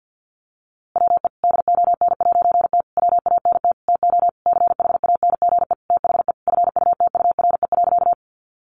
Morse code, heard as a keyed tone, about 50 wpm, 700 Hz. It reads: WE DON'T WANT TO CHANGE THE FUTURE.
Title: A Sound Of Thunder 50wpm